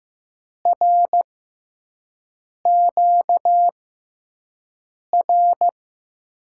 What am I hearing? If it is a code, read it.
Morse code, heard as a keyed tone, about 15 wpm, spaced as for 5 wpm, 700 Hz. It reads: RQ R
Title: USA Call Signs - 6x Character Spacing - 15wpm